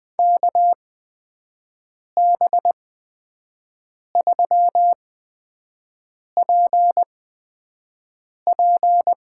Morse code, heard as a keyed tone, about 20 wpm, spaced as for 5 wpm, 700 Hz. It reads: KB3PP
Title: USA Call Signs - 6 Characters - 8x Character Spacing - 20wpm